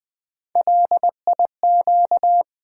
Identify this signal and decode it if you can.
Morse code, heard as a keyed tone, about 20 wpm, 700 Hz. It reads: LIQ
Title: Build A Word - 3 Letters At A Time - 6-Letter Words - 20wpm